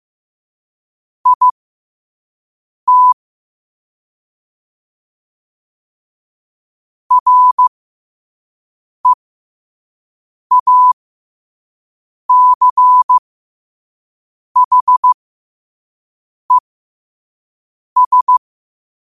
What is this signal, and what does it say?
Morse code, heard as a keyed tone, about 15 wpm, spaced as for 5 wpm, 1000 Hz. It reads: IT REACHES
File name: qotdfiles / 5wpm